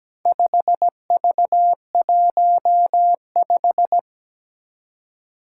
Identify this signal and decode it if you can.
Morse code, heard as a keyed tone, about 17 wpm, 700 Hz. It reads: 5V15
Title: Four Challenging Letter-Number Combinations 17wpm